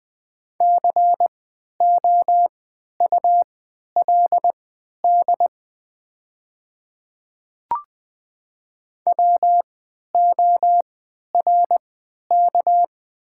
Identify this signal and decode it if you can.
Morse code, heard as a keyed tone, about 20 wpm, spaced as for 11 wpm, 700 Hz. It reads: COULD WORK
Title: 3x Character Spacing - Top 300 Words - 20wpm